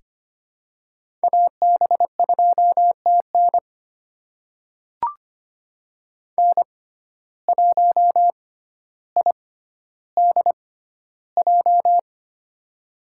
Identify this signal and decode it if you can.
Morse code, heard as a keyed tone, about 25 wpm, spaced as for 13 wpm, 700 Hz. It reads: AB2TN N1IDJ
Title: USA Call Signs - 5 Characters - 6x Character Spacing - 25wpm